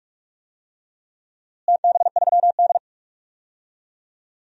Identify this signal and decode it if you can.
Morse code, heard as a keyed tone, about 46 wpm, 700 Hz. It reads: TB3D
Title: DXCC-100 Call Signs - 4 Characters - Repeated 2x - 45wpm